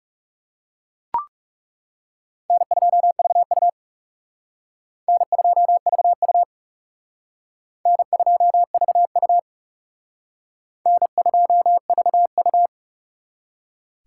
Morse code, heard as a keyed tone, about 37 wpm, 700 Hz. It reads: N2VU N2VU N2VU N2VU
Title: Speed Racing - USA Call Signs 30wpm